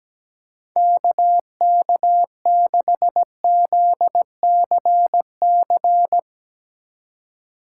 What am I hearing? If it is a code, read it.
Morse code, heard as a keyed tone, about 18 wpm, 700 Hz. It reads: KK6ZCC
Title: Speed Racing - USA 2x3 Call Signs 17wpm